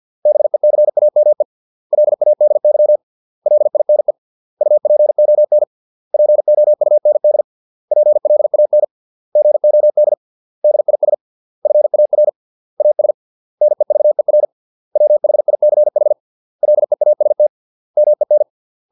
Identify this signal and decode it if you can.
Morse code, heard as a keyed tone, about 50 wpm, 600 Hz. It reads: BECAME LADY LINE UPON POUND PLAN GOD DIS FAR AS NEVER WHICH LEAST MEN